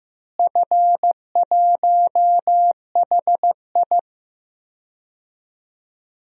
Morse code, heard as a keyed tone, about 15 wpm, 700 Hz. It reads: F1HI
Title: DXCC-100 Call Signs - 4 Characters - Repeated 3x - 15wpm